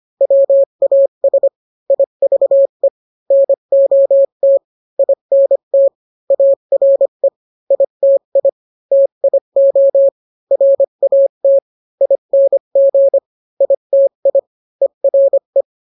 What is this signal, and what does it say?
Morse code, heard as a keyed tone, about 25 wpm, 550 Hz. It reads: WAS IVE NOT INT ARE ITI TIO RAT ING ITI ERE